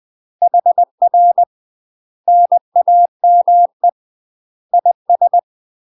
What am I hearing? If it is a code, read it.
Morse code, heard as a keyed tone, about 20 wpm, 700 Hz. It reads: HR NAME IS
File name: UK MASTER QSOs TRAINING / 20WPM / UK MASTER QSOs 0046